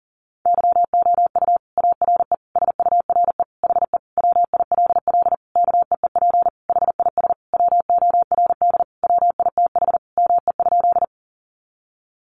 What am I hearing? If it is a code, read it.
Morse code, heard as a keyed tone, about 40 wpm, 700 Hz. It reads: YOU ARE SURE HE WILL KEEP HIS WORD WITH ME?
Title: Sentences from Top 300 Words 40wpm